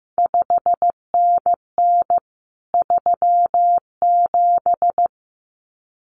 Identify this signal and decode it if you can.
Morse code, heard as a keyed tone, about 15 wpm, 700 Hz. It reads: 5NN 37
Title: CQ WW DX Contest Exchange Encore - 15wpm